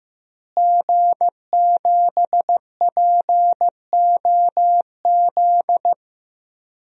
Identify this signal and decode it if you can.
Morse code, heard as a keyed tone, about 15 wpm, 700 Hz. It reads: G7POZ
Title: DXCC-100 Call Signs - 5 Characters - Repeated 3x - 15wpm